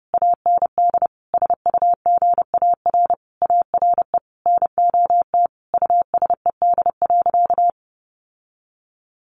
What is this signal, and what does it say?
Morse code, heard as a keyed tone, about 28 wpm, 700 Hz. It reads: AND SUGAR ARE NOT USED.
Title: Sentences from Top 1500 Words 30wpm